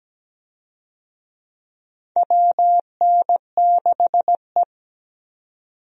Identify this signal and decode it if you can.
Morse code, heard as a keyed tone, about 17 wpm, 700 Hz. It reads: WN6E
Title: USA Call Signs - 4 Characters - 17wpm